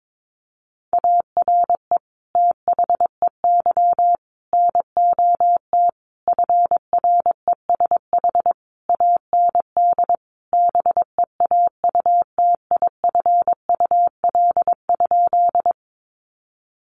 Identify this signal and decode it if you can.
Morse code, heard as a keyed tone, about 22 wpm, 700 Hz. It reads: ARE THEY NOT FRESH AND BEAUTIFUL?
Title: Sentences from Top 1200 Words 22wpm